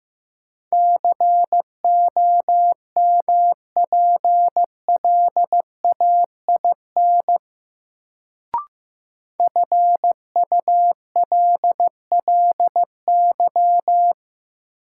Morse code, heard as a keyed tone, about 16 wpm, 700 Hz. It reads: COMPLAIN FULLY
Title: Sentences from Top 1300 Words - New Word Drill - 15wpm